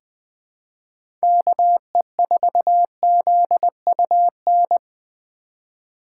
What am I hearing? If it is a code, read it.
Morse code, heard as a keyed tone, about 20 wpm, 700 Hz. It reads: KE4ZUN
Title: 2023 POTA Hunter Call Signs - 20wpm